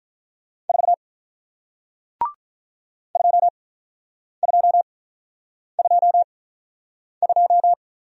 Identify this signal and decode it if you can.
Morse code, heard as a keyed tone, about 48 wpm, spaced as for 9 wpm, 700 Hz. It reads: 4 2222